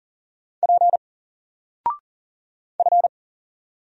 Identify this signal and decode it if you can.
Morse code, heard as a keyed tone, about 40 wpm, 700 Hz. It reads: P F
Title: Single Letter-Number Alt 40wpm